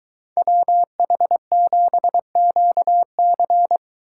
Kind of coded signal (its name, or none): Morse code